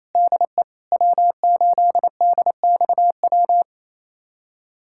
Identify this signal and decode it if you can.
Morse code, heard as a keyed tone, about 28 wpm, 700 Hz. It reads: DE W8DXW
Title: USA Callsign DE Callsign - 5 Characters - 28wpm